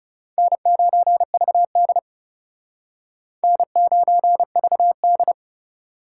Morse code, heard as a keyed tone, about 39 wpm, 700 Hz. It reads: N9VD N9VD